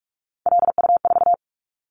Morse code, heard as a keyed tone, about 46 wpm, 700 Hz. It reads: LU4
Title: Callsign Prefixes for Top 100 DXCC - 45wpm